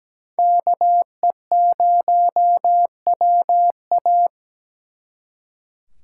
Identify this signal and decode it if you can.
Morse code, heard as a keyed tone, about 20 wpm, 700 Hz. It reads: KE0WA
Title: Speed Racing - USA 2x2 Call Signs 17wpm